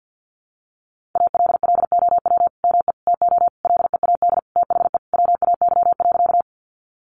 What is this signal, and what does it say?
Morse code, heard as a keyed tone, about 50 wpm, 700 Hz. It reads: ALLOW ME TO LEAD THE WAY.
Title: Sentences from Top 700 Words 50wpm